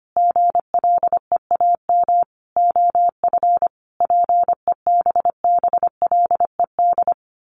Morse code, heard as a keyed tone, about 25 wpm, 700 Hz. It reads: GLEAM OF PEBBLED
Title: A Sound Of Thunder 25wpm